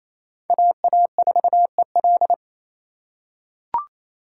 Morse code, heard as a keyed tone, about 28 wpm, 700 Hz. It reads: AA4EL
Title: USA Call Signs - 5 Characters - 4x Character Spacing - 28wpm